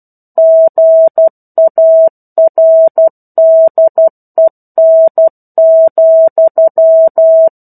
Morse code, heard as a keyed tone, about 12 wpm, 650 Hz. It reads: GARDEN,
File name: 01-Chapter-01 12 WPM